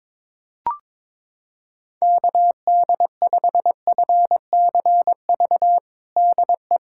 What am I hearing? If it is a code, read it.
Morse code, heard as a keyed tone, about 21 wpm, 700 Hz. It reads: KD5FCV DE
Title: USA Callsign DE Callsign 22wpm